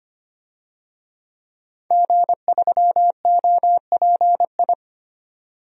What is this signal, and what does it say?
Morse code, heard as a keyed tone, about 25 wpm, 700 Hz. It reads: G3OPI
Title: DXCC-100 Call Signs - 5 Characters - 25wpm